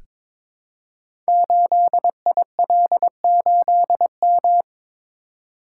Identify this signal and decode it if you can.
Morse code, heard as a keyed tone, about 22 wpm, 700 Hz. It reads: 8IL8M
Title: Code Groups - 5 Letter:Numbers - 22wpm